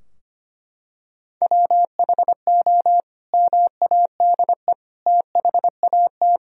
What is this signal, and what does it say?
Morse code, heard as a keyed tone, about 25 wpm, 700 Hz. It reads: WHO MADE THAT
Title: Sentences from Top 1000 Words - Review Part 2 - 25wpm